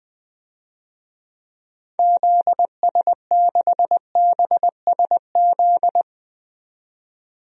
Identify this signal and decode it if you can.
Morse code, heard as a keyed tone, about 20 wpm, 700 Hz. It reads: ZS6BSZ
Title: DXCC-100 Call Signs - 6 Characters - 20wpm